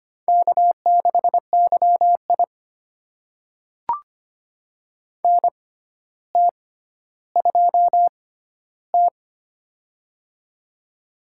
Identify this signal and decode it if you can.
Morse code, heard as a keyed tone, about 25 wpm, spaced as for 13 wpm, 700 Hz. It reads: K6YI NT2T
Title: USA Call Signs - 4 Characters - 6x Character Spacing - 25wpm